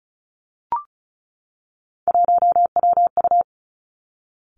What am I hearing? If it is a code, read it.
Morse code, heard as a keyed tone, about 36 wpm, 700 Hz. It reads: 1WU
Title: Three Challenging Letter-Number Combinations 35wpm